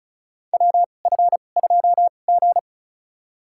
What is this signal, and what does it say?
Morse code, heard as a keyed tone, about 36 wpm, spaced as for 25 wpm, 700 Hz. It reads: WF2G
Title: USA Call Signs - 2x Character Spacing - 35wpm